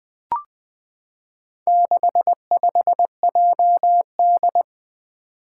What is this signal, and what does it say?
Morse code, heard as a keyed tone, about 20 wpm, 700 Hz. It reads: 65JD
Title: Four Challenging Letter-Number Combinations 20wpm